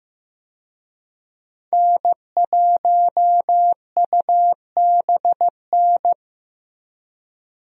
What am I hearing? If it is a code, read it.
Morse code, heard as a keyed tone, about 15 wpm, 700 Hz. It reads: N1UBN
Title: USA Call Signs - 5 Characters - Repeated 2x - 15wpm